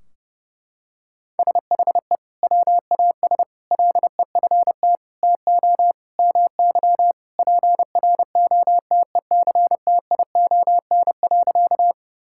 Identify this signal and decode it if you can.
Morse code, heard as a keyed tone, about 30 wpm, 700 Hz. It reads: SHE WAS LEFT TO MY PROTECTION.
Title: Sentences from Top 1600 Words 30wpm